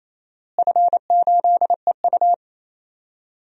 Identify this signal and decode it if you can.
Morse code, heard as a keyed tone, about 28 wpm, 700 Hz. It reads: F8EU
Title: DXCC-100 Call Signs - 4 Characters - Repeated 2x - 28wpm